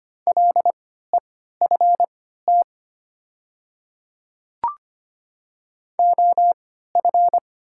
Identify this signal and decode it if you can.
Morse code, heard as a keyed tone, about 25 wpm, spaced as for 14 wpm, 700 Hz. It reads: LEFT OF
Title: 3x Character Spacing - Top 300 Words - 25wpm